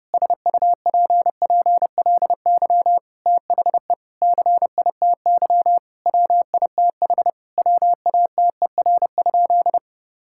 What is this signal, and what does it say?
Morse code, heard as a keyed tone, about 30 wpm, 700 Hz. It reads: SUPPLY THE CITY WITH WATER?
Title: Sentences from Top 1000 Words - Review Part 5 - 30wpm